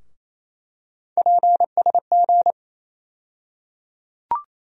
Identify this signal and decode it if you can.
Morse code, heard as a keyed tone, about 28 wpm, 700 Hz. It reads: PSG
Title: USA Call Sign Postfixes - 3 Letters 28wpm